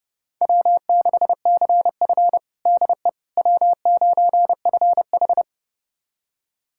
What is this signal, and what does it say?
Morse code, heard as a keyed tone, about 30 wpm, 700 Hz. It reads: W6CF DE W9FH
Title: USA Callsign DE Callsign - 4 Characters - 30wpm